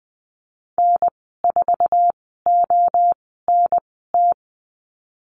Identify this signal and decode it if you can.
Morse code, heard as a keyed tone, about 20 wpm, spaced as for 15 wpm, 700 Hz. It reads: N4ONT
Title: 055 Lesson 10g - Letter r - Call Signs 20wpm